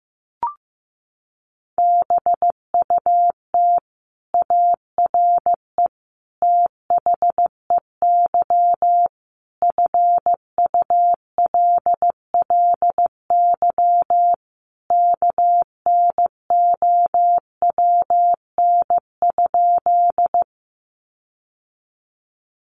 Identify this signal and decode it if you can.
Morse code, heard as a keyed tone, about 15 wpm, 700 Hz. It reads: BUT ARE THEY FULLY KNOWN?
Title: Sentences from Top 1300 Words 15wpm